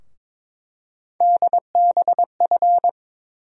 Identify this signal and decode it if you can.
Morse code, heard as a keyed tone, about 21 wpm, 700 Hz. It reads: DBF
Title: USA Call Sign Postfixes - 3 Letters 22wpm